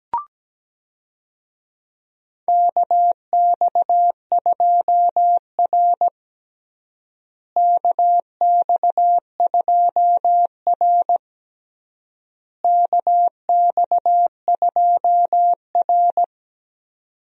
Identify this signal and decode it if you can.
Morse code, heard as a keyed tone, about 17 wpm, 700 Hz. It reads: KX2R KX2R KX2R
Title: USA Call Signs - 4 Characters - Repeated 3x - 17wpm